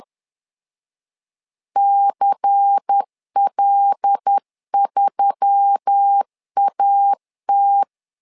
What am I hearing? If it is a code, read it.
Morse code, heard as a keyed tone, about 10 wpm, 800 Hz. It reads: CL3AT
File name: LEZIONI / 5 LBVC / 5 LBVC 10wpm 2349